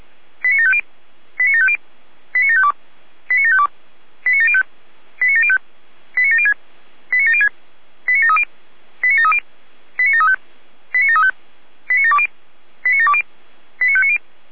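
Five-tone selective call (ZVEI), heard as quick runs of five tones, tones of about 70 ms. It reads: 89750 89750 89752 89752 89705 89705 89706 89706 89730 89730 89735 89735 89720 89720 89590
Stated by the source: ZVEI five-tone selective call received on an SDR.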